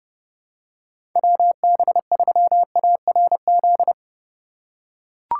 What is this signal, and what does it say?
Morse code, heard as a keyed tone, about 30 wpm, 700 Hz. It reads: WB3ARZ
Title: USA Call Signs - 6 Characters - 3x Character Spacing - 30wpm